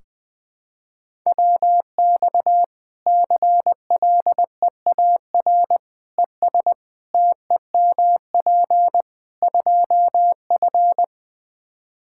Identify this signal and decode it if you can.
Morse code, heard as a keyed tone, about 21 wpm, 700 Hz. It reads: WX CLEAR ES TEMP 2F
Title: Practice QSO - Exchange 2 - Info - 20wpm